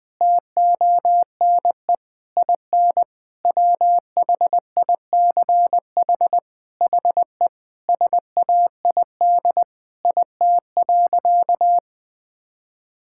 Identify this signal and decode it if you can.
Morse code, heard as a keyed tone, about 20 wpm, 700 Hz. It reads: TONE IN WHICH HE SAID IT.
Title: Sentences from Top 1600 Words 20wpm